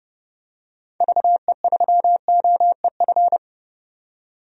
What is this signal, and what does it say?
Morse code, heard as a keyed tone, about 32 wpm, 700 Hz. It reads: VE3OEF
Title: DXCC-100 Call Signs - 6 Characters - 30wpm